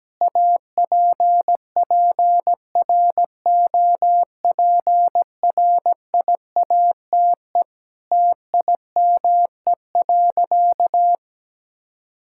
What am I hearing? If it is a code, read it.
Morse code, heard as a keyed tone, about 17 wpm, 700 Hz. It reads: APPROPRIATE TIME.
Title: Sentences from Top 1000 Words - Review Part 5 - 17wpm